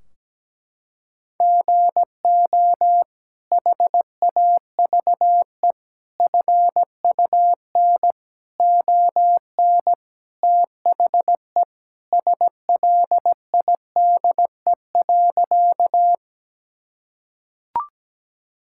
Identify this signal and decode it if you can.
Morse code, heard as a keyed tone, about 17 wpm, 700 Hz. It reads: GO HAVE FUN ON THE SLIDE.
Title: Sentences from Top 1600 Words 17wpm